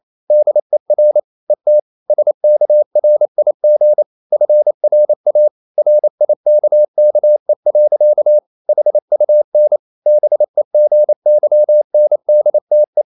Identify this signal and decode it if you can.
Morse code, heard as a keyed tone, about 28 wpm, 600 Hz. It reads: DER ET SKRIG FRA RIKKE. HUN BEGYNDTE
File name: Ferie QRQ 28wpm